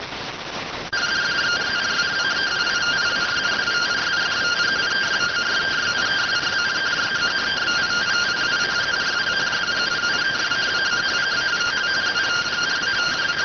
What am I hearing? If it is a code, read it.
Two-tone teleprinter signal, about 170 Hz shift.